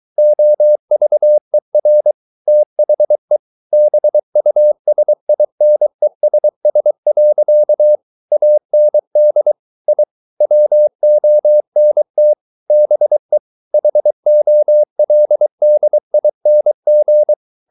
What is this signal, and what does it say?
Morse code, heard as a keyed tone, about 23 wpm, 600 Hz. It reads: OVER THE BUSINESS. AND I WONT BE HOLDING